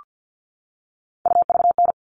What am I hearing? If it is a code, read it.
Morse code, heard as a keyed tone, about 48 wpm, 700 Hz. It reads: UVN
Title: All Cut Numbers - 3-Digit 50wpm